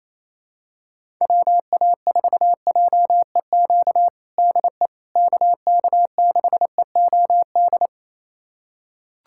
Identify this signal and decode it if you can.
Morse code, heard as a keyed tone, about 28 wpm, 700 Hz. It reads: WA4JEQ DE KK6EOD
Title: USA Callsign DE Callsign 28wpm